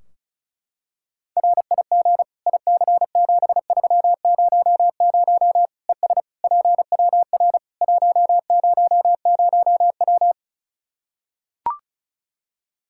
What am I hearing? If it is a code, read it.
Morse code, heard as a keyed tone, about 36 wpm, 700 Hz. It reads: RIG IC7300 ES PWR 100W
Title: Practice QSO - Exchange 2 - Info - 35wpm